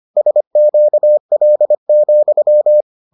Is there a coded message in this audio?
Yes: Morse code from 0.2 to 2.8 s, heard as a keyed tone, about 25 wpm, 600 Hz. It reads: SQL,